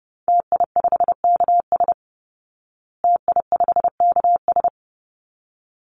Morse code, heard as a keyed tone, about 30 wpm, 700 Hz. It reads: TI5KS TI5KS
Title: Call Signs Repeated 3x - 30wpm